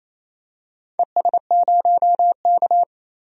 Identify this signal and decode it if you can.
Morse code, heard as a keyed tone, about 28 wpm, 700 Hz. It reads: ES0K
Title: DXCC-100 Call Signs - 4 Characters - 28wpm